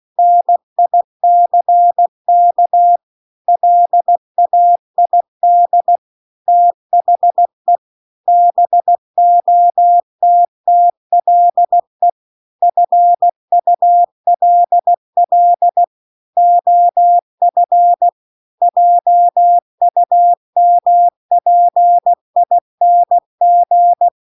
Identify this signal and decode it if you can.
Morse code, heard as a keyed tone, about 16 wpm, 700 Hz. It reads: NICK LAID THE BOTTLE FULL OF JUMPING
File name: CW-The-Big-Twoheaded-River-2-16WPM